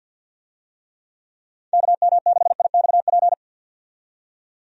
Morse code, heard as a keyed tone, about 48 wpm, 700 Hz. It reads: KM6IXP
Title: USA Call Signs - 6 Characters - 50wpm